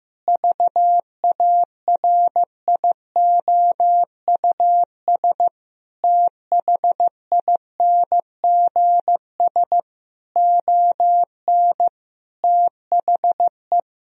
Morse code, heard as a keyed tone, about 15 wpm, 700 Hz. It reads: VARIOUS THINGS ON THE
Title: Sentences from Top 1000 Words - Review Part 4 - 15wpm